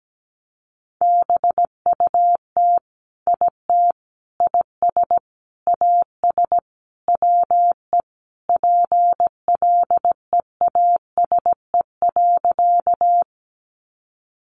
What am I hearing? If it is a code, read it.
Morse code, heard as a keyed tone, about 17 wpm, 700 Hz. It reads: BUT IT IS AS WE PLEASE.
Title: Sentences from Top 1500 Words 17wpm